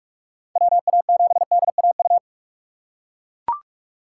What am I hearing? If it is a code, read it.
Morse code, heard as a keyed tone, about 41 wpm, 700 Hz. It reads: WA7DAU